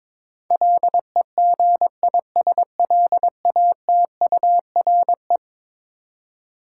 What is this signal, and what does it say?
Morse code, heard as a keyed tone, about 22 wpm, 700 Hz. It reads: LEGISLATURE
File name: practice / 22wpm / Eleven Letter Words - 22wpm